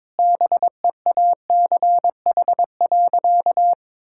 Morse code, heard as a keyed tone, about 22 wpm, 700 Hz. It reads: BEACH.
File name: Difficult Sentences with Hints - 22wpm